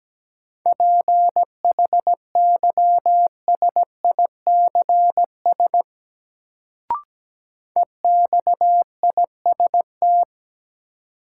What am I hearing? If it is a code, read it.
Morse code, heard as a keyed tone, about 17 wpm, 700 Hz. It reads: PHYSICS EXIST
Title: Sentences from Top 1000 Words - Review Part 5 - New Word Drill - 17wpm